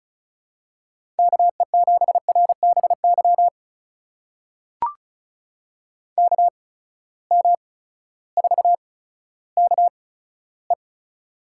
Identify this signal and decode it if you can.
Morse code, heard as a keyed tone, about 36 wpm, spaced as for 16 wpm, 700 Hz. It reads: KE7RBY KM4KE
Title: USA Call Signs - 6 Characters - 8x Character Spacing - 35wpm